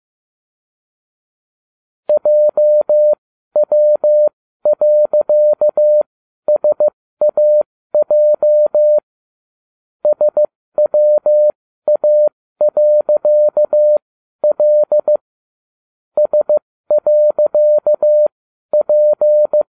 Morse code, heard as a keyed tone, about 15 wpm, spaced as for 12 wpm, 600 Hz. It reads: JW.SAJ SWA.L S.P